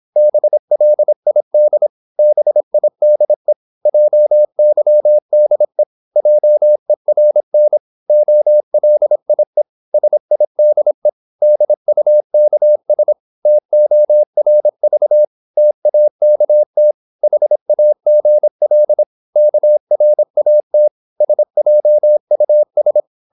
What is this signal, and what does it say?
Morse code, heard as a keyed tone, about 26 wpm, 600 Hz. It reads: BLID BIDE JYDE JERN OLIE SIDE DUKS TORV TAKT HAGL KRAT SJUS
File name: Korte ord DK 26wpm